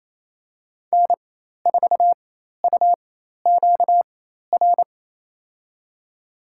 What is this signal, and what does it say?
Morse code, heard as a keyed tone, about 27 wpm, spaced as for 13 wpm, 700 Hz. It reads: N4UQR
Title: USA Call Signs - 4x Character Spacing - 28wpm